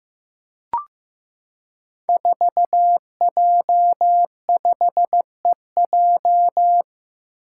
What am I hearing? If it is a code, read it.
Morse code, heard as a keyed tone, about 15 wpm, 700 Hz. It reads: 4J5EJ
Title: Five Challenging Letter-Number Combinations 15wpm